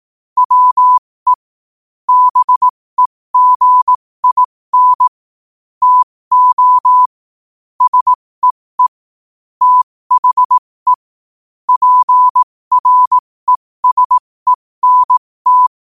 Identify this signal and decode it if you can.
Morse code, heard as a keyed tone, about 18 wpm, spaced as for 15 wpm, 1000 Hz. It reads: WE BEGIN TO SEE THE PRESENT
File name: qotdfiles / 15wpm